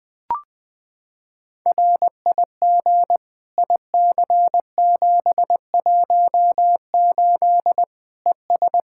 Morse code, heard as a keyed tone, about 20 wpm, 700 Hz. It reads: RIG IC718 ES
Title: K1USN SST QSO Practice 20wpm